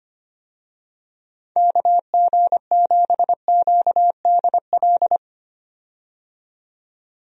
Morse code, heard as a keyed tone, about 25 wpm, 700 Hz. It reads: KG7QDL
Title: USA Call Signs - 6 Characters - 25wpm